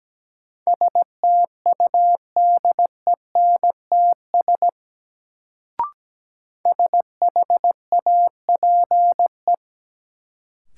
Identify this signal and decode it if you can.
Morse code, heard as a keyed tone, about 17 wpm, 700 Hz. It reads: STUDENTS SHAPE
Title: Sets of 5 Words Encore - New Word Drill - 17wpm